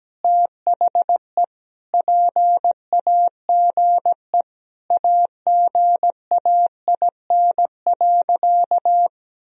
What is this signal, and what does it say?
Morse code, heard as a keyed tone, about 17 wpm, 700 Hz. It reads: THE PAGE AGAIN.
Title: Sentences from Top 1000 Words - Review Part 1 - 17wpm